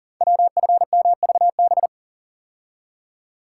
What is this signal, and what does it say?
Morse code, heard as a keyed tone, about 40 wpm, 690 Hz. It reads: WFMVB
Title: Code Groups - 5 Letters - 40wpm